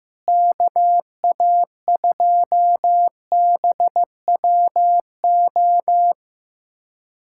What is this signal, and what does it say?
Morse code, heard as a keyed tone, about 15 wpm, 700 Hz. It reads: KA2BWO
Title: USA Callsign DE Callsign - 6 Characters - 15wpm